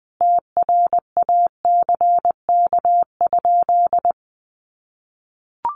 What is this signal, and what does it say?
Morse code, heard as a keyed tone, about 20 wpm, 700 Hz. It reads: TRACK?
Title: Sentences from Top 1100 Words 20wpm